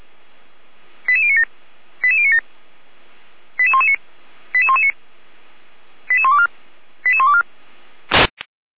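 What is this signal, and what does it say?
Five-tone selective call (ZVEI), heard as quick runs of five tones, tones of about 70 ms. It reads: 88097 88097 88119 88119 88124 88124
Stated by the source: ZVEI five-tone selective call 88057, received on SDR.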